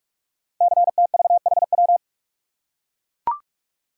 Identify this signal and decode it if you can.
Morse code, heard as a keyed tone, about 44 wpm, 700 Hz. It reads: CTVHW
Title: Code Groups - 5 Letters - 45wpm